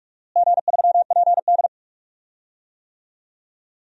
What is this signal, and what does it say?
Morse code, heard as a keyed tone, about 41 wpm, 700 Hz. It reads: G3PD
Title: DXCC-100 Call Signs - 4 Characters - Repeated 2x - 45wpm